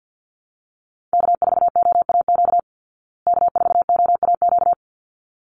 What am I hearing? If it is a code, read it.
Morse code, heard as a keyed tone, about 50 wpm, 700 Hz. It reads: K4OAQ K4OAQ
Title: Call Signs Repeated 3x - 50wpm